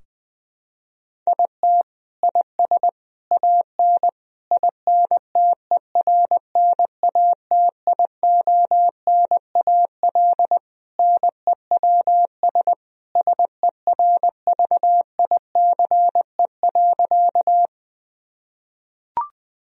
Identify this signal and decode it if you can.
Morse code, heard as a keyed tone, about 20 wpm, 700 Hz. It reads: IT IS AN INTERNATIONAL NEWS SERVICE.
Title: Sentences from Top 1000 Words - Review Part 4 - 20wpm